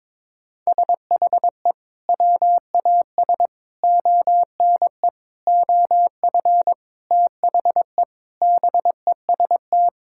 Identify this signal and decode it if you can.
Morse code, heard as a keyed tone, about 22 wpm, 700 Hz. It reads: SHE WAS ONE OF THE BEST
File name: practice / 22wpm / Sentences with 4-Word Phrases - 22wpm